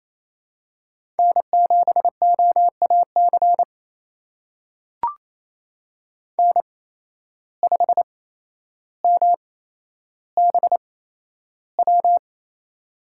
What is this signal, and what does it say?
Morse code, heard as a keyed tone, about 27 wpm, spaced as for 12 wpm, 700 Hz. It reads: N7OAC N5MBW
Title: USA Call Signs - 5 Characters - 8x Character Spacing - 28wpm